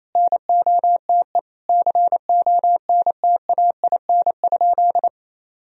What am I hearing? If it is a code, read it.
Morse code, heard as a keyed tone, about 28 wpm, 700 Hz. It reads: NOTE CONTAIN?
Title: Sentences from Top 1000 Words - Review Part 2 - 28wpm